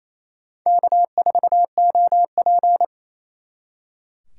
Morse code, heard as a keyed tone, about 35 wpm, 700 Hz. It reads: K4OP
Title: Speed Racing - USA Call Signs 28wpm